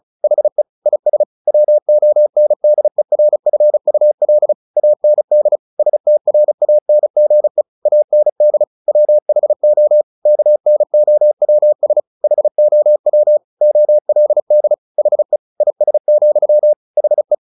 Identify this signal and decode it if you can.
Morse code, heard as a keyed tone, about 34 wpm, 600 Hz. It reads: HE IS WONDERFUL AND STRANGE AND WHO KNOWS HOW OLD HE IS, HE
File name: the old man and the sea / the old man and the sea 1 35wpm